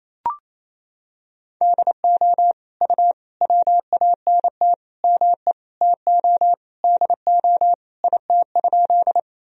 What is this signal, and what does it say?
Morse code, heard as a keyed tone, about 28 wpm, 700 Hz. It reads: DO U WANT ME TO DO IT?
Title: Sentences from Top 200 Words and Abbreviations 28wpm